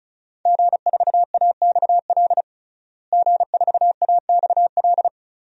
Morse code, heard as a keyed tone, about 34 wpm, 700 Hz. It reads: G4AXL G4AXL
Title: DXCC-100 Call Signs - 5 Characters - Repeated 3x - 35wpm